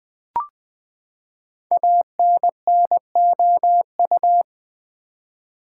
Morse code, heard as a keyed tone, about 20 wpm, 700 Hz. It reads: ANNOU
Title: Build A Word - 5 Letters At A Time - 12-Letter Words - 20wpm